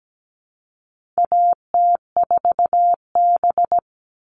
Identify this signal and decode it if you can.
Morse code, heard as a keyed tone, about 17 wpm, 700 Hz. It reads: AT4B
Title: All Cut Numbers - 4-Digit 17wpm